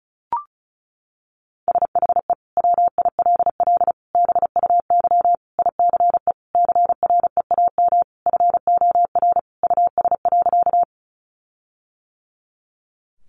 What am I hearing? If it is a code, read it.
Morse code, heard as a keyed tone, about 34 wpm, 700 Hz. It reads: SHE WILL BUY ICE CREAM FOR US.
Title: Sentences from Top 1500 Words 35wpm